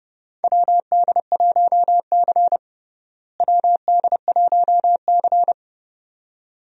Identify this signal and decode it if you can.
Morse code, heard as a keyed tone, about 32 wpm, 700 Hz. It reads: WD1C WD1C
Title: USA Call Signs - 4 Characters - Repeated 2x - 30wpm